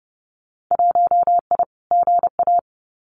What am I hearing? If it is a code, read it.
Morse code, heard as a keyed tone, about 30 wpm, 700 Hz. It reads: 1I GA
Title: Winter Field Day Exchanges Encore 30wpm